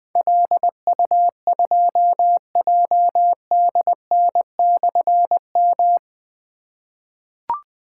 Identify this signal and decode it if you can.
Morse code, heard as a keyed tone, about 21 wpm, 700 Hz. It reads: LU2JDN/M
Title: Difficult Call Signs 20wpm